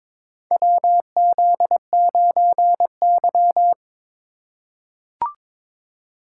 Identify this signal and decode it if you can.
Morse code, heard as a keyed tone, about 22 wpm, 700 Hz. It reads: WZ9Y
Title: USA Call Signs - 4 Characters - 6x Character Spacing - 22wpm